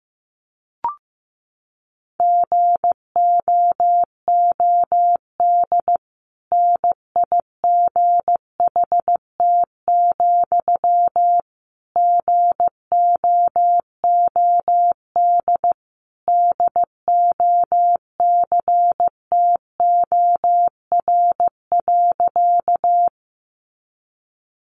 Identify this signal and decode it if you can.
Morse code, heard as a keyed tone, about 15 wpm, 700 Hz. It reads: GOOD NIGHT, GOOD DOCTOR.
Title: Sentences from Top 1500 Words 15wpm